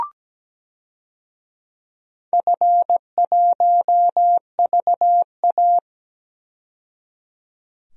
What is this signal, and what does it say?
Morse code, heard as a keyed tone, about 17 wpm, 700 Hz. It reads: F1VA
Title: DXCC-100 Call Signs - 4 Characters - 17wpm